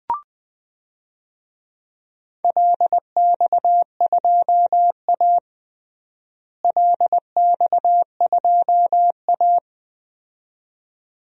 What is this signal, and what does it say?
Morse code, heard as a keyed tone, about 20 wpm, 700 Hz. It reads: LX2A LX2A
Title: DXCC-100 Call Signs - 4 Characters - Repeated 2x - 20wpm